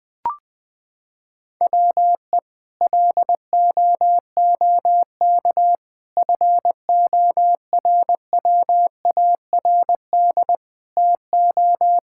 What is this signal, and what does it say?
Morse code, heard as a keyed tone, about 21 wpm, 700 Hz. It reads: WE LOOK FORWARD TO
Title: Sentences with 6-Word Phrases - 20wpm